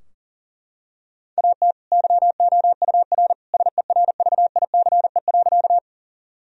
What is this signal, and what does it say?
Morse code, heard as a keyed tone, about 40 wpm, 700 Hz. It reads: AT YOUR SERVICE.
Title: Sentences from Top 1000 Words - Review Part 4 - 40wpm